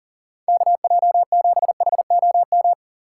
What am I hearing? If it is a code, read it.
Morse code, heard as a keyed tone, about 39 wpm, 700 Hz. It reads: KJ7HOM
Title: USA Call Signs - 2x Character Spacing - 40wpm